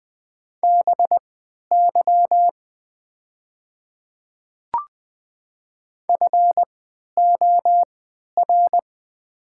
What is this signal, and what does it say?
Morse code, heard as a keyed tone, about 20 wpm, spaced as for 11 wpm, 700 Hz. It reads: BY FOR
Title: 3x Character Spacing - Top 100 Words - 20wpm